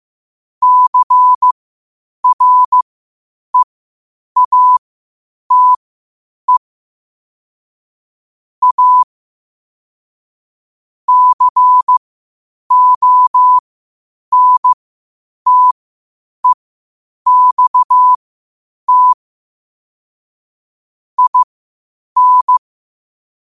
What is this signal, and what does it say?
Morse code, heard as a keyed tone, about 15 wpm, spaced as for 8 wpm, 1000 Hz. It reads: CREATE A CONTEXT IN
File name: qotdfiles / 8wpm